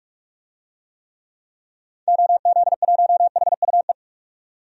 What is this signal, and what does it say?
Morse code, heard as a keyed tone, about 46 wpm, 700 Hz. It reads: OZ1HUE
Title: DXCC-100 Call Signs - 6 Characters - 45wpm